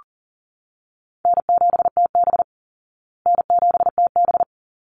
Morse code, heard as a keyed tone, about 40 wpm, 700 Hz. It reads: N7TB N7TB
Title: Easy Call Signs Repeated 2x - 40wpm